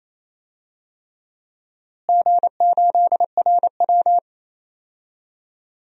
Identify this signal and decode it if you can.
Morse code, heard as a keyed tone, about 28 wpm, 700 Hz. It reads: G8RW
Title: DXCC-100 Call Signs - 4 Characters - Repeated 3x - 28wpm